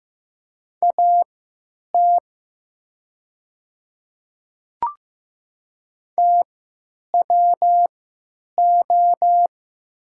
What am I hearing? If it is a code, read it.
Morse code, heard as a keyed tone, about 15 wpm, spaced as for 9 wpm, 700 Hz. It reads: AT TWO
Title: 3x Character Spacing - Top 200 Words - 15wpm